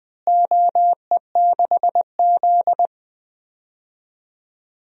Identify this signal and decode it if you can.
Morse code, heard as a keyed tone, about 20 wpm, 700 Hz. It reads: OE6Z
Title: DXCC-100 Call Signs - 4 Characters - Repeated 2x - 20wpm